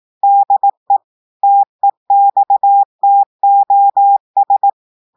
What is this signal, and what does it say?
Morse code, heard as a keyed tone, about 18 wpm, 800 Hz. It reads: DE TEXTOS
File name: audios / portugais / portugais texte en clair 1 vitesse 18 mots minutes